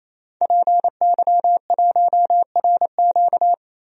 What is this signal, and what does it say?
Morse code, heard as a keyed tone, about 28 wpm, 700 Hz. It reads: PY1RQ
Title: DXCC-100 Call Signs - 5 Characters - 28wpm